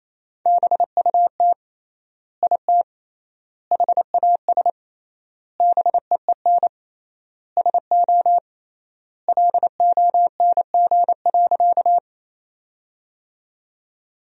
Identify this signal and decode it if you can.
Morse code, heard as a keyed tone, about 28 wpm, 700 Hz. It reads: BUT IT HAS BEEN SO LONG.
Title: 3x Word-Spacing - Sentences from Top 100 Words - 28wpm